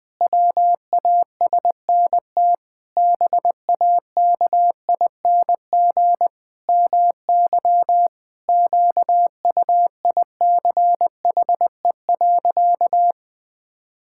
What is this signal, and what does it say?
Morse code, heard as a keyed tone, about 20 wpm, 700 Hz. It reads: WASNT BAKING MY QUICHE.
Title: Pangram Challenge v2 - 20wpm